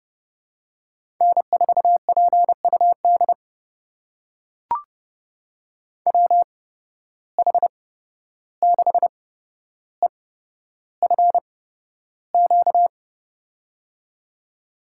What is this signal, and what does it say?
Morse code, heard as a keyed tone, about 32 wpm, spaced as for 12 wpm, 700 Hz. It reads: N4PUD WH6EFQ
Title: USA Call Signs - 8x Character Spacing - 30wpm